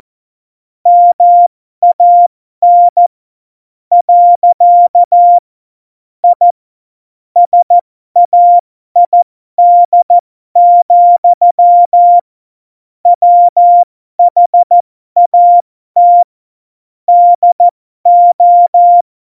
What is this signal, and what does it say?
Morse code, heard as a keyed tone, about 14 wpm, spaced as for 12 wpm, 700 Hz. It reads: MAN . I SAID, WHAT DO
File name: Morsekurs / Witze